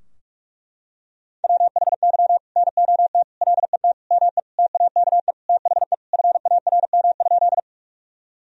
Morse code, heard as a keyed tone, about 41 wpm, 700 Hz. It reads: WHY NOT LET ME TAKE THE FARM?
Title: Sentences from Top 1000 Words - Review Part 2 - 45wpm